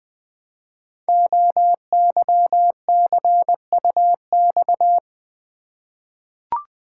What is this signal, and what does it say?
Morse code, heard as a keyed tone, about 20 wpm, 700 Hz. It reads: OYCUX
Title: Code Groups - 5 Letter:Numbers - 20wpm